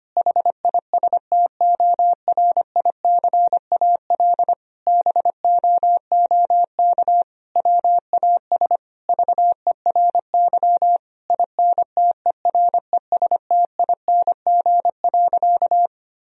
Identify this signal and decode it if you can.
Morse code, heard as a keyed tone, about 25 wpm, 700 Hz. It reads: HISTORICAL BOOK WAS VERY INTERESTING.
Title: Sentences from Top 1000 Words - Review Part 4 - 25wpm